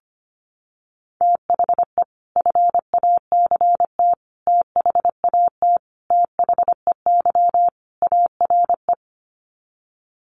Morse code, heard as a keyed tone, about 25 wpm, 700 Hz. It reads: THE FACT THAT THEY ARE
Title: Sets of 5 Words Encore2 25wpm